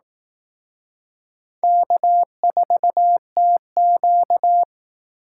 Morse code, heard as a keyed tone, about 18 wpm, 700 Hz. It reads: K4TQ
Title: Speed Racing - USA 1x2 Call Signs 15wpm